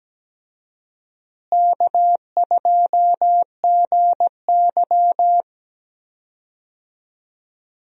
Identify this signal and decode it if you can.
Morse code, heard as a keyed tone, about 17 wpm, 700 Hz. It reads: K2GY
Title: USA Call Signs - 4 Characters - Repeated 2x - 17wpm